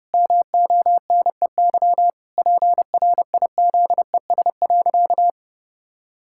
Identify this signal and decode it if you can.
Morse code, heard as a keyed tone, about 30 wpm, 700 Hz. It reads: MONEY PRIZES.
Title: Pangram Challenge v2 - 30wpm